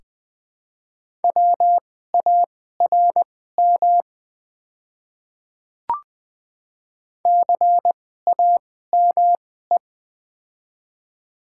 Morse code, heard as a keyed tone, about 20 wpm, spaced as for 14 wpm, 690 Hz. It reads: WARM CAME
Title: 2x Character Spacing - Top 500 Words - 20wpm